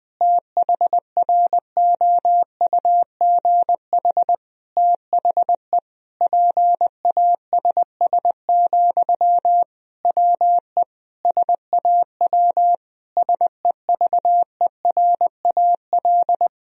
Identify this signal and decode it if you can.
Morse code, heard as a keyed tone, about 20 wpm, 700 Hz. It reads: THROUGH THE PASS, WE SAW SEVERAL
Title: Sentences from Top 1000 Words - Review Part 2 - 20wpm